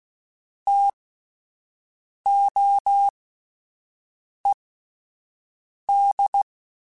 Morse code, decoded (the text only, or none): TOED